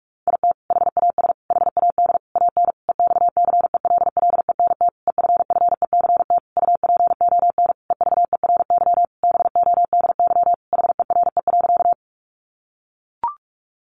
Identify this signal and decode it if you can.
Morse code, heard as a keyed tone, about 44 wpm, 700 Hz. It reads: IT HAS HAD AN EXCELLENT EFFECT UPON EVERY BODY HERE.
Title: Sentences from Top 1100 Words 45wpm